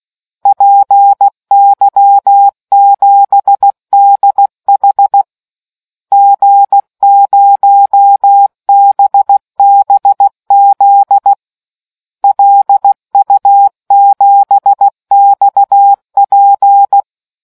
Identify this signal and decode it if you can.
Morse code, heard as a keyed tone, about 16 wpm, 800 Hz. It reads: PY7DH G0BBZ LU7XP